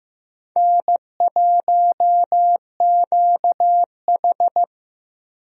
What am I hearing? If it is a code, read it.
Morse code, heard as a keyed tone, about 16 wpm, 700 Hz. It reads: N1QH
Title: USA Call Signs - 4 Characters - Repeated 3x - 15wpm